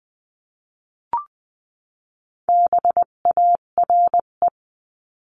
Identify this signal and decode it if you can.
Morse code, heard as a keyed tone, about 20 wpm, 700 Hz. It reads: BARE
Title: 258 Lesson 42c - Words 16-20wpm